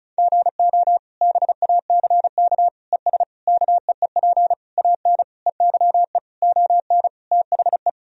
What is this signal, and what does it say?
Morse code, heard as a keyed tone, about 36 wpm, 700 Hz. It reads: GO BACK ES KEEP AN EYE ON THE
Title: Sentences from Top 300 Words and Abbreviations 35wpm